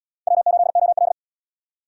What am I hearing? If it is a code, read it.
Morse code, heard as a keyed tone, about 78 wpm, 700 Hz. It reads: W8OL